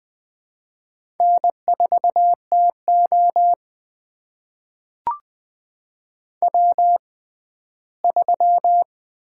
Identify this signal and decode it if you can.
Morse code, heard as a keyed tone, about 20 wpm, spaced as for 14 wpm, 700 Hz. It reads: N4TO W3
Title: USA Call Signs - 4 Characters - 6x Character Spacing - 20wpm